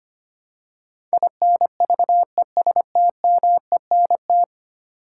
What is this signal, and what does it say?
Morse code, heard as a keyed tone, about 25 wpm, 700 Hz. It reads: INVESTMENT
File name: Ten Letter Words - 25wpm